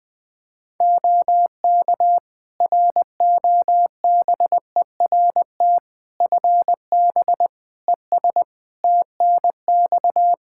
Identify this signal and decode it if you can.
Morse code, heard as a keyed tone, about 20 wpm, 700 Hz. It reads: OK ROBERT FB ES TNX
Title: K1USN SST QSO Practice 20wpm